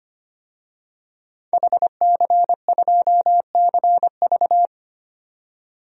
Morse code, heard as a keyed tone, about 25 wpm, 700 Hz. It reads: HC2CV
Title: DXCC-100 Call Signs - 5 Characters - 25wpm